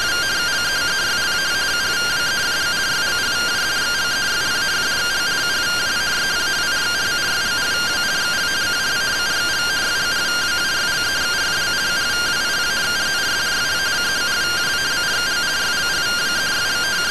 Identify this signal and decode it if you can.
Two-tone teleprinter signal, about 170 Hz shift.